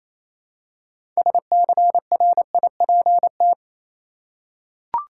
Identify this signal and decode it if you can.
Morse code, heard as a keyed tone, about 27 wpm, 700 Hz. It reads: SCRIPT
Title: Six Letter Words - 28wpm